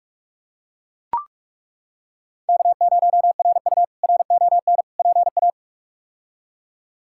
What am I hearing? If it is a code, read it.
Morse code, heard as a keyed tone, about 42 wpm, 700 Hz. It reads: K0RU RON PA
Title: K1USN SST QSO Practice - 45wpm